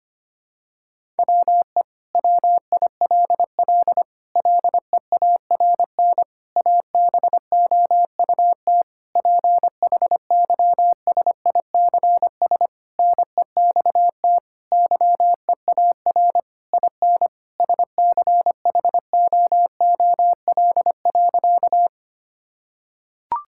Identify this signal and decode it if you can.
Morse code, heard as a keyed tone, about 25 wpm, 700 Hz. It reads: WE WILL LEARN ABOUT PHYSICS NEXT YEAR IN SCHOOL.
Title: Sentences from Top 1000 Words - Review Part 5 - 25wpm